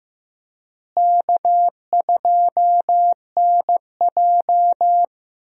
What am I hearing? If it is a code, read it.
Morse code, heard as a keyed tone, about 15 wpm, 700 Hz. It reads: K2NJ
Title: USA Call Signs - 4 Characters - Repeated 2x - 15wpm